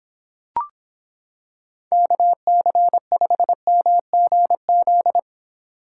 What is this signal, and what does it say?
Morse code, heard as a keyed tone, about 26 wpm, 700 Hz. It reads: KC5MGZ